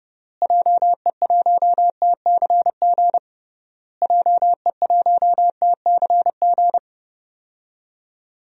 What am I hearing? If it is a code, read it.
Morse code, heard as a keyed tone, about 30 wpm, 700 Hz. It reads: JE1TCG JE1TCG
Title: DXCC-100 Call Signs - 6 Characters - Repeated 3x - 30wpm